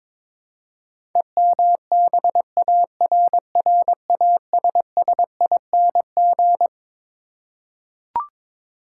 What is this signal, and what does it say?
Morse code, heard as a keyed tone, about 21 wpm, 700 Hz. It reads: EMBARRASSING
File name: practice / 22wpm / Twelve Letter Words - 22wpm